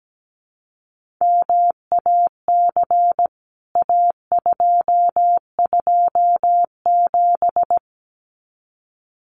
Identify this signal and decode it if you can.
Morse code, heard as a keyed tone, about 17 wpm, 700 Hz. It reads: MAC A227
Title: CWT Exchange 17wpm